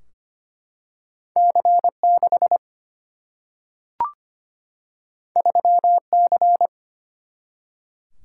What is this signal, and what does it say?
Morse code, heard as a keyed tone, about 25 wpm, 700 Hz. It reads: C6 3C
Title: Single-Letter Single-Number 25wpm